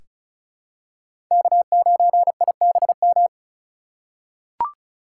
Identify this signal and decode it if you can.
Morse code, heard as a keyed tone, about 34 wpm, 700 Hz. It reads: K9IBM
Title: K1USN SST QSO Practice - 35wpm